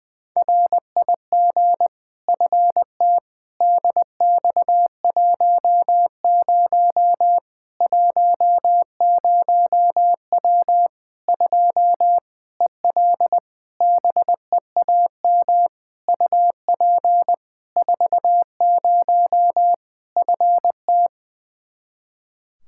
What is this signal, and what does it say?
Morse code, heard as a keyed tone, about 20 wpm, 700 Hz. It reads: RIG FT DX10 10W 2 EL BEAM UP 40 FT
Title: Practice QSO - Exchange 2 - Info - 20wpm